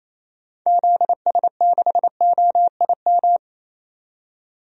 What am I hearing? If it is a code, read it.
Morse code, heard as a keyed tone, about 28 wpm, 700 Hz. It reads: ZS6OIM
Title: DXCC-100 Call Signs - 6 Characters - 28wpm